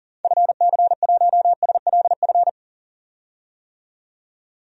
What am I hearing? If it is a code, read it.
Morse code, heard as a keyed tone, about 37 wpm, 700 Hz. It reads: FC1SLF
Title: DXCC-100 Call Signs - 6 Characters - 40wpm